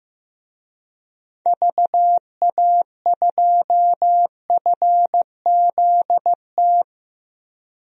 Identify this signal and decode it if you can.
Morse code, heard as a keyed tone, about 15 wpm, 700 Hz. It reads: VA2FZT
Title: DXCC-100 Call Signs - 6 Characters - 15wpm